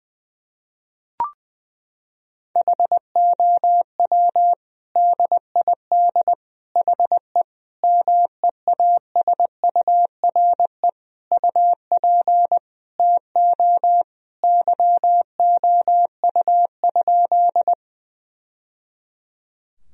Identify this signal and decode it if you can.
Morse code, heard as a keyed tone, about 20 wpm, 700 Hz. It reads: HOW DID HE MEASURE UP TO YOU?
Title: Sentences from Top 1000 Words - Review Part 2 - 20wpm